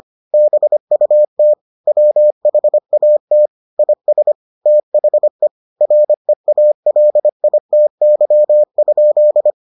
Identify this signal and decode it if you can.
Morse code, heard as a keyed tone, about 25 wpm, 600 Hz. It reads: BUT WHAT IS THE REALITY?